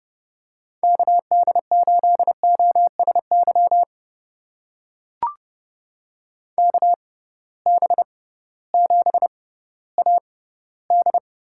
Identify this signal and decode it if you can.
Morse code, heard as a keyed tone, about 30 wpm, spaced as for 16 wpm, 700 Hz. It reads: KD8OSY KB7AD
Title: USA Call Signs - 6 Characters - 6x Character Spacing - 30wpm